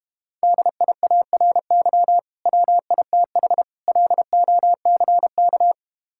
Morse code, heard as a keyed tone, about 32 wpm, 700 Hz. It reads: DIARY WITH LOCK